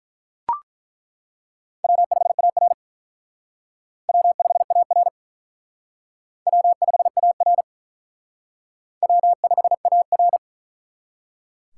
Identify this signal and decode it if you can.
Morse code, heard as a keyed tone, about 42 wpm, 700 Hz. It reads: W5AR W5AR W5AR W5AR
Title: Speed Racing - USA 1x2 Call Signs 35wpm